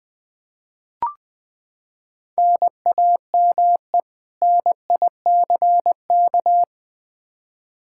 Morse code, heard as a keyed tone, about 20 wpm, 700 Hz. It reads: NAME NICK
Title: Practice QSO - Exchange 1 - Beginning - 20wpm